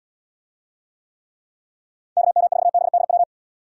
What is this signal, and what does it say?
Morse code, heard as a keyed tone, about 78 wpm, 700 Hz. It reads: KM4BDW